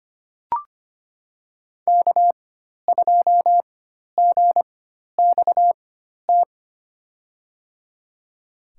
Morse code, heard as a keyed tone, about 25 wpm, spaced as for 12 wpm, 700 Hz. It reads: K2GXT
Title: USA Call Signs - 5 Characters - 4x Character Spacing - 25wpm